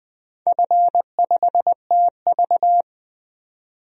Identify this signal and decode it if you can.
Morse code, heard as a keyed tone, about 21 wpm, 700 Hz. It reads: F5TV
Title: DXCC-100 Call Signs - 4 Characters - 20wpm